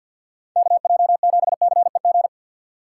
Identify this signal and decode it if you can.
Morse code, heard as a keyed tone, about 59 wpm, 700 Hz. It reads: KJ7CEG
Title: Speed Racing - USA Call Signs 50wpm